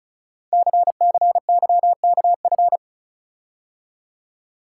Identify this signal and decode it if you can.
Morse code, heard as a keyed tone, about 36 wpm, 700 Hz. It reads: CCYKF
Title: Code Groups - 5 Letters - 35wpm